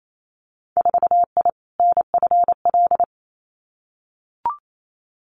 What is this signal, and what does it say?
Morse code, heard as a keyed tone, about 28 wpm, 700 Hz. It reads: 4I NFL
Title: Winter Field Day Exchanges 28wpm